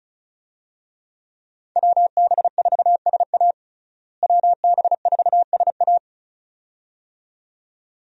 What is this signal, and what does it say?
Morse code, heard as a keyed tone, about 36 wpm, 700 Hz. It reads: WB4SA WB4SA
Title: USA Call Signs - 5 Characters - Repeated 2x - 35wpm